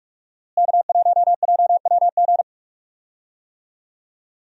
Morse code, heard as a keyed tone, about 45 wpm, 700 Hz. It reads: K1JWG
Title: USA Call Signs - 5 Characters - Repeated 3x - 45wpm